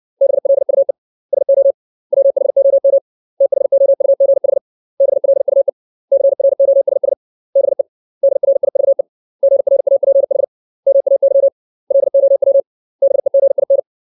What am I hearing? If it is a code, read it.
Morse code, heard as a keyed tone, about 60 wpm, 550 Hz. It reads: BLUE SO WHOM THOUGH BLUE CROSS BE DRIVE GRAPH MAY LOW BEGIN
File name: 632 words / 632 Words 60WPM